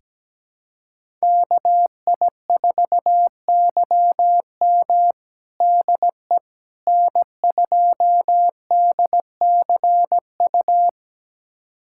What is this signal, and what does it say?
Morse code, heard as a keyed tone, about 17 wpm, 700 Hz. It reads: KI4YM DE N2DCU
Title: USA Callsign DE Callsign - 5 Characters - 17wpm